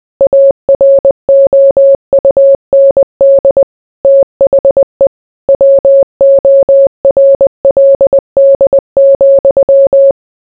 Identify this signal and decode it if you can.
Morse code, heard as a keyed tone, about 20 wpm, 550 Hz. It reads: AROUND THE WORLD,